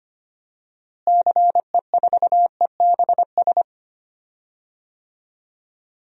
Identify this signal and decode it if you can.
Morse code, heard as a keyed tone, about 25 wpm, 700 Hz. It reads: CE4EBS
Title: DXCC-100 Call Signs - 6 Characters - Repeated 2x - 25wpm